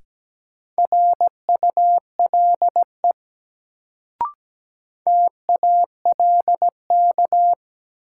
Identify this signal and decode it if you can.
Morse code, heard as a keyed tone, about 17 wpm, 700 Hz. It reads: RULE TALK
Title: Sentences from Top 400 Words - New Word Drill - 17wpm